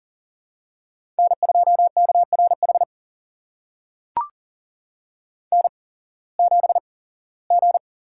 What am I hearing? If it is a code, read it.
Morse code, heard as a keyed tone, about 40 wpm, spaced as for 21 wpm, 700 Hz. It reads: N2KRH N7G
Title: USA Call Signs - 5 Characters - 8x Character Spacing - 40wpm